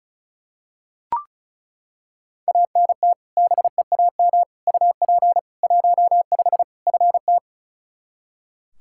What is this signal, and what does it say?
Morse code, heard as a keyed tone, about 36 wpm, 700 Hz. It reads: ANT BEAM UP 15 FT
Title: Practice QSO - Exchange 2 - Info - 35wpm